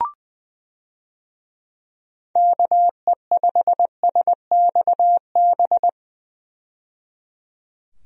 Morse code, heard as a keyed tone, about 21 wpm, 700 Hz. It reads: KE5SXB
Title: USA Call Signs - 6 Characters - 20wpm